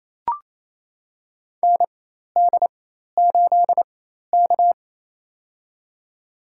Morse code, heard as a keyed tone, about 28 wpm, spaced as for 13 wpm, 700 Hz. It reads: ND8K
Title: USA Call Signs - 4 Characters - 4x Character Spacing - 28wpm